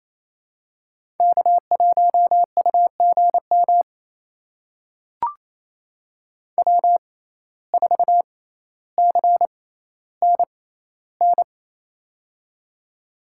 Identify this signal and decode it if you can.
Morse code, heard as a keyed tone, about 28 wpm, spaced as for 14 wpm, 700 Hz. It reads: K1UGM W4CNN
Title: USA Call Signs - 5 Characters - 6x Character Spacing - 28wpm